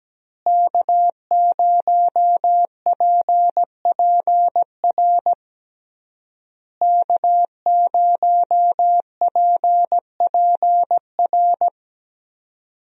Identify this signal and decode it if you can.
Morse code, heard as a keyed tone, about 17 wpm, 700 Hz. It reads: K0PPR K0PPR
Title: USA Call Signs - 5 Characters - Repeated 3x - 17wpm